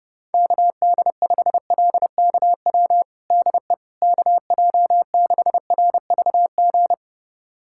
Morse code, heard as a keyed tone, about 30 wpm, 700 Hz. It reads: KD5LKW DE KJ6RVG
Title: USA Callsign DE Callsign - 6 Characters - 30wpm